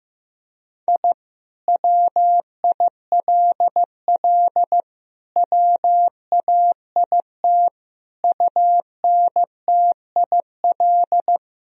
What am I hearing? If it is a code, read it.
Morse code, heard as a keyed tone, about 15 wpm, 700 Hz. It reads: I WILL WAIT UNTIL
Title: Sentences from Top 1000 Words - Review Part 1 - 15wpm